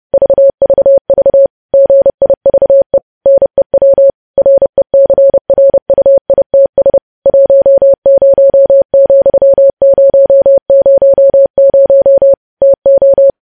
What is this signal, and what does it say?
Morse code, heard as a keyed tone, about 30 wpm, 550 Hz. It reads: VVV GIVE NEW RECRUITS 10,000 TO
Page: news headlines in morse code at 30 wpm